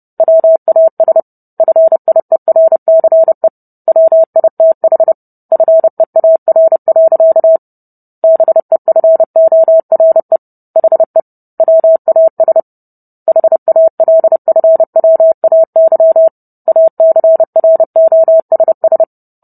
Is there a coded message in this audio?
Yes: Morse code from 0.2 to 19.0 s, heard as a keyed tone, about 30 wpm, 650 Hz. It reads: WAS FIERCE WITH FEAR. BEFORE HE WAS HALFWAY ACROSS